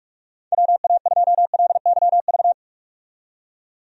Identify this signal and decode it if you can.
Morse code, heard as a keyed tone, about 41 wpm, 700 Hz. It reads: WA2LYV
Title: USA Call Signs - 6 Characters - 45wpm